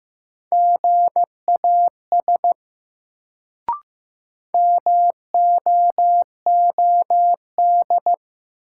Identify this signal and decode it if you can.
Morse code, heard as a keyed tone, about 15 wpm, 700 Hz. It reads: GAS MOOD
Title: Sentences from Top 1100 Words - New Word Drill Alt - 15wpm